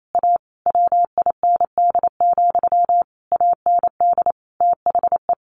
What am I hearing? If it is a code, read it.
Morse code, heard as a keyed tone, about 28 wpm, 700 Hz. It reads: A WIND, AND THE
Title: A Sound Of Thunder 28wpm